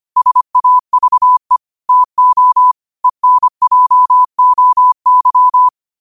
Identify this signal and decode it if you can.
Morse code, heard as a keyed tone, about 25 wpm, 1000 Hz. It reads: SAVE TO ENJOY